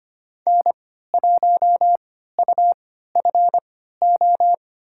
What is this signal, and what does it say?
Morse code, heard as a keyed tone, about 25 wpm, spaced as for 14 wpm, 700 Hz. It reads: N1UFO
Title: USA Call Signs - 3x Character Spacing - 25wpm